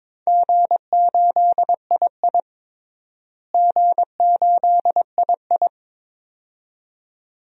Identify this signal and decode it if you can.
Morse code, heard as a keyed tone, about 22 wpm, 700 Hz. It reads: G8II G8II
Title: DXCC-100 Call Signs - 4 Characters - Repeated 2x - 22wpm